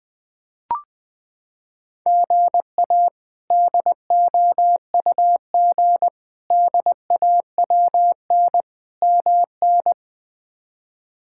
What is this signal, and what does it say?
Morse code, heard as a keyed tone, about 20 wpm, 700 Hz. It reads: GA DOUG DAWN MN
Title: K1USN SST QSO Practice - 20wpm